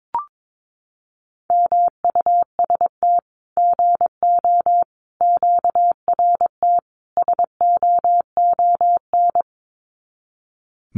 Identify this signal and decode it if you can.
Morse code, heard as a keyed tone, about 22 wpm, 700 Hz. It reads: MUST GO QRT SOON
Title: Common QSO Phrases 22wpm